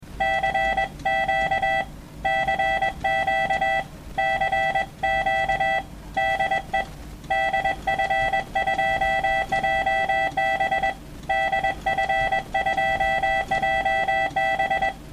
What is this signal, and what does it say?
Morse code, heard as a keyed tone, about 21 wpm, 780 Hz, with noise. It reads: CQ CQ CQ DE DF2JB DF2JB
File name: morsen-Dateien / cq de df2jb